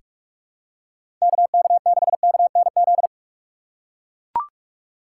Morse code, heard as a keyed tone, about 42 wpm, 700 Hz. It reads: KK6KNZ
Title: USA Call Signs - 6 Characters - 3x Character Spacing - 45wpm